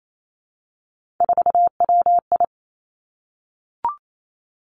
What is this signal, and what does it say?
Morse code, heard as a keyed tone, about 28 wpm, 700 Hz. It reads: 4WI
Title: Three Challenging Letter-Number Combinations 28wpm